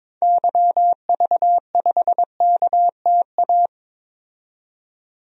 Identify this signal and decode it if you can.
Morse code, heard as a keyed tone, about 22 wpm, 700 Hz. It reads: YV5KTA
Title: DXCC-100 Call Signs - 6 Characters - 22wpm